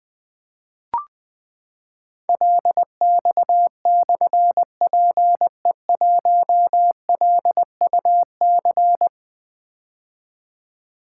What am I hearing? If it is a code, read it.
Morse code, heard as a keyed tone, about 20 wpm, 700 Hz. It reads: LX/PE1LUC
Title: Difficult Call Signs 20wpm